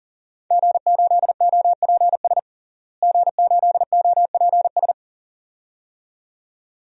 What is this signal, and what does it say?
Morse code, heard as a keyed tone, about 40 wpm, 700 Hz. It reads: G8OPS G8OPS
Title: DXCC-100 Call Signs - 5 Characters - Repeated 2x - 40wpm